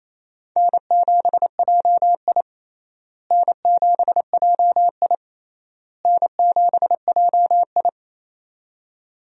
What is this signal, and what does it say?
Morse code, heard as a keyed tone, about 28 wpm, 700 Hz. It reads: N7JI N7JI N7JI
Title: USA Call Signs - 4 Characters - Repeated 3x - 28wpm